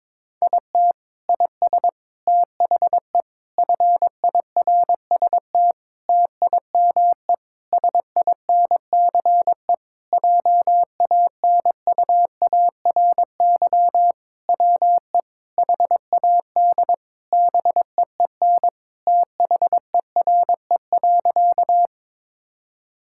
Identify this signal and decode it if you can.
Morse code, heard as a keyed tone, about 22 wpm, 700 Hz. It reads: IT IS THE FIRST TIME SINCE JANUARY WE HAD BEEN THERE.
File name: practice / 22wpm / Sentences with 4-Word Phrases - 22wpm